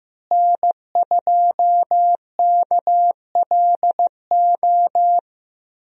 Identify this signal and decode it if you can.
Morse code, heard as a keyed tone, about 15 wpm, 700 Hz. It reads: N2KLO
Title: USA Call Signs - 5 Characters - Repeated 3x - 15wpm